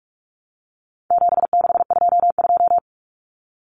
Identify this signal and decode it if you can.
Morse code, heard as a keyed tone, about 44 wpm, 700 Hz. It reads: 7622
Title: Common Cut Numbers - 4-Digit 45wpm